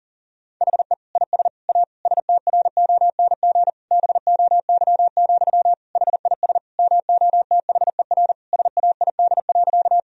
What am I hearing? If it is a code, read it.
Morse code, heard as a keyed tone, about 40 wpm, 700 Hz. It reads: HE IS A STRONG BOY, HIS MOTHER SAID.
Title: Sentences from Top 1000 Words - Review Part 3 - 40wpm